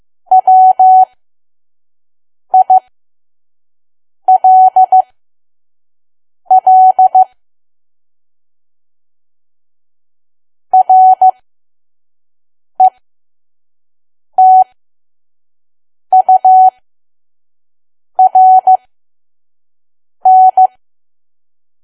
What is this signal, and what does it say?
Morse code, heard as a keyed tone, about 15 wpm, spaced as for 5 wpm, 750 Hz. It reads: WILL RETURN
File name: Archive / 5 WPM files / PM